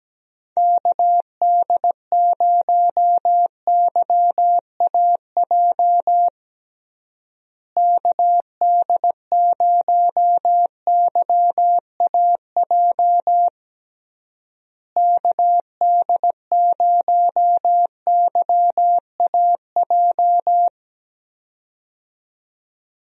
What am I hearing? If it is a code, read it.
Morse code, heard as a keyed tone, about 17 wpm, 700 Hz. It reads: KD0YAJ KD0YAJ KD0YAJ
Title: USA Call Signs - 6 Characters - Repeated 3x - 17wpm